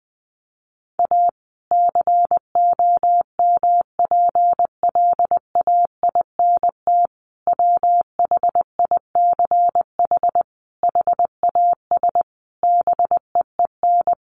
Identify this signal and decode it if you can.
Morse code, heard as a keyed tone, about 20 wpm, 700 Hz. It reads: A COMPLAINT WHICH HAS BEEN
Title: Sentences from Top 1400 Words 20wpm